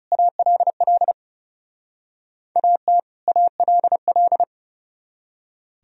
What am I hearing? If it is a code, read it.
Morse code, heard as a keyed tone, about 36 wpm, 700 Hz. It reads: ALL AT ALL
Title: Speed-Racing - Sets of 2 Words 30wpm